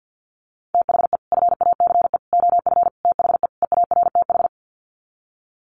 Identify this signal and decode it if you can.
Morse code, heard as a keyed tone, about 50 wpm, 700 Hz. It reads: THE FACE OF THE EARTH
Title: Sets of 5 Words Encore 50wpm